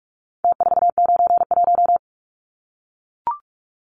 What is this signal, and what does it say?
Morse code, heard as a keyed tone, about 44 wpm, 700 Hz. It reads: T491
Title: Numbers - Contest Style - 4-Digit 45wpm